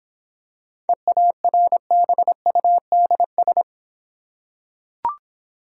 Morse code, heard as a keyed tone, about 26 wpm, 690 Hz. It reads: EARBUDS